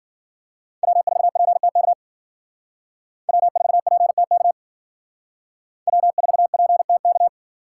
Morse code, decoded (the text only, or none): W4PTK W4PTK W4PTK